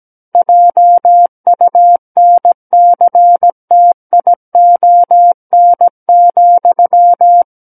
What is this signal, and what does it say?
Morse code, heard as a keyed tone, about 17 wpm, 700 Hz. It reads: JUNCTION,
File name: Book One / 08-Chapter 8 - 17 WPM